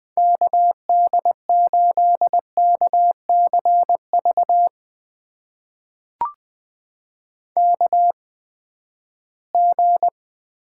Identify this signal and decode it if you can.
Morse code, heard as a keyed tone, about 20 wpm, spaced as for 14 wpm, 700 Hz. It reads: KD8KCV KG
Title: USA Call Signs - 6 Characters - 8x Character Spacing - 20wpm